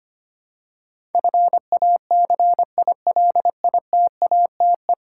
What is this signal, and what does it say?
Morse code, heard as a keyed tone, about 25 wpm, 700 Hz. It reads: FACILITATE
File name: Ten Letter Words - 25wpm